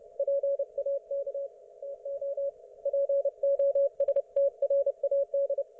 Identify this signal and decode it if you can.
Morse code, heard as a keyed tone, about 30 wpm, 560 Hz. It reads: PAK TO POSTRAD
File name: Archiv / CUC208 / ok1fhi cul208